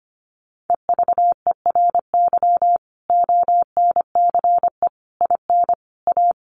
Morse code, heard as a keyed tone, about 25 wpm, 700 Hz. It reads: EVERY ONCE IN A
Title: Sets of 5 Words 25wpm